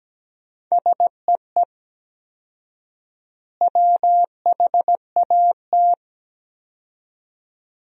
Morse code, heard as a keyed tone, about 17 wpm, 700 Hz. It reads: SEE WHAT
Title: 4x Word-Spacing - Sentences from Top 500 Words - 17wpm